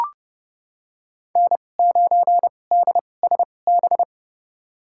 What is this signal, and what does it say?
Morse code, heard as a keyed tone, about 31 wpm, spaced as for 22 wpm, 700 Hz. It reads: N9DSB
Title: USA Call Signs - 5 Characters - 2x Character Spacing - 30wpm